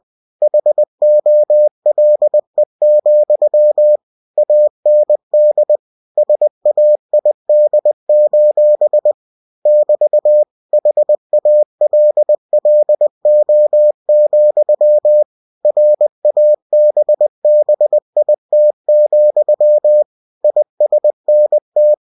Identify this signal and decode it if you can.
Morse code, heard as a keyed tone, about 20 wpm, 600 Hz. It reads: HOLE, AND SAID: <BT> HALLO, RABBIT, ISNT
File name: cw-books / winnie-the-pooh / winnie-20wpm-page-0014